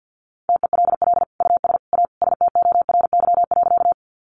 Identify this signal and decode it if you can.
Morse code, heard as a keyed tone, about 49 wpm, 700 Hz. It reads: TELL US A STORY.
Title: Sentences from Top 300 Words 50wpm